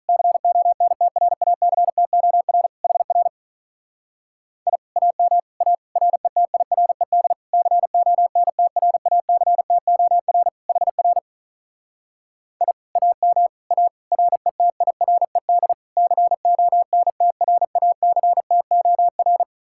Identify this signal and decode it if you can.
Morse code, heard as a keyed tone, about 44 wpm, 700 Hz. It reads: CONTRACTOR HR I AM A RETIRED CONTRACTOR HR I AM A RETIRED CONTRACTOR